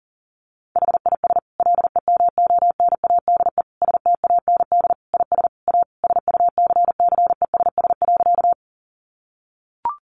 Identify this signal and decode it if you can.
Morse code, heard as a keyed tone, about 37 wpm, 700 Hz. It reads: HIS LEMONADE STAND IS A SUCCESS.
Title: Additional Easy Phrases 40wpm